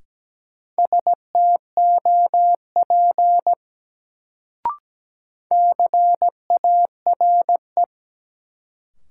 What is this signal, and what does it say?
Morse code, heard as a keyed tone, about 17 wpm, 700 Hz. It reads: STOP CARE
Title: Sentences from Top 300 Words - New Word Drill - 17wpm